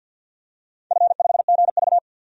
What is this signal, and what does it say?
Morse code, heard as a keyed tone, about 51 wpm, 700 Hz. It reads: F5GV
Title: DXCC-100 Call Signs - 4 Characters - 50wpm